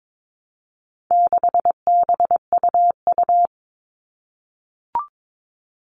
Morse code, heard as a keyed tone, about 22 wpm, 700 Hz. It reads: 6BUU
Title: All Cut Numbers - 4-Digit 22wpm